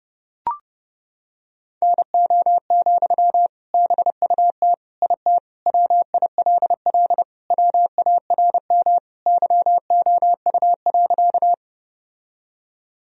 Morse code, heard as a keyed tone, about 30 wpm, 700 Hz. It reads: NO, BUT IT WILL WARM YOU.
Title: Sentences from Top 1000 Words - Review Part 2 - 30wpm